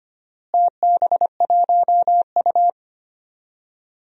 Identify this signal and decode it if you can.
Morse code, heard as a keyed tone, about 25 wpm, 700 Hz. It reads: TB1U
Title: DXCC-100 Call Signs - 4 Characters - Repeated 2x - 25wpm